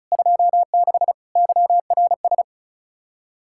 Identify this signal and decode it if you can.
Morse code, heard as a keyed tone, about 43 wpm, 700 Hz. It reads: 26 YRS
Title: Speed-Racing Common QSO Phrases Encore - 35wpm